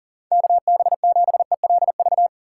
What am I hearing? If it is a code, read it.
Morse code, heard as a keyed tone, about 40 wpm, 700 Hz. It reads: KB7ELV
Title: USA Call Signs - 6 Characters - 4x Character Spacing - 40wpm